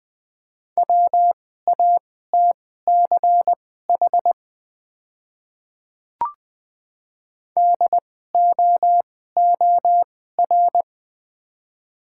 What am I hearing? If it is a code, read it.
Morse code, heard as a keyed tone, about 20 wpm, spaced as for 14 wpm, 700 Hz. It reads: WATCH DOOR
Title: 2x Character Spacing - Top 300 Words - 20wpm